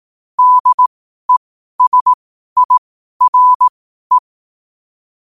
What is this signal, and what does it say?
Morse code, heard as a keyed tone, about 18 wpm, spaced as for 12 wpm, 1000 Hz. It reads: DESIRE
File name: qotdfiles / 12wpm